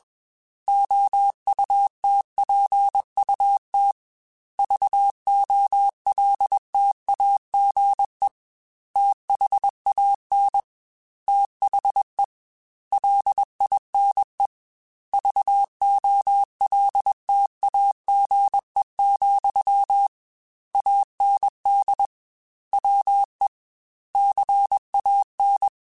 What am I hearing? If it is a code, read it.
Morse code, heard as a keyed tone, about 21 wpm, 800 Hz. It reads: OUTPUT VOLTAGE THAN THE LINE VOLTAGE, AND WE CAN
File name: Technical Subjects / Technical Subjects-mp3s / Sw Pwr Sup @20